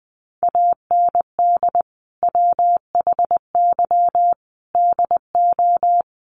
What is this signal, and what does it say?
Morse code, heard as a keyed tone, about 20 wpm, 700 Hz. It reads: AND WHY DO
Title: Sentences from Top 1100 Words 20wpm